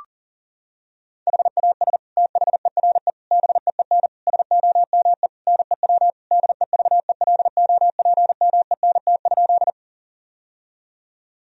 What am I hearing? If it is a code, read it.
Morse code, heard as a keyed tone, about 40 wpm, 700 Hz. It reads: HAS THERE BEEN SOME NEW DEVELOPMENT?
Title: Sentences from Top 1000 Words - Review Part 4 - 40wpm